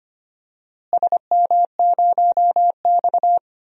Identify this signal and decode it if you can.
Morse code, heard as a keyed tone, about 25 wpm, 700 Hz. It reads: SM0X
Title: DXCC-100 Call Signs - 4 Characters - Repeated 3x - 25wpm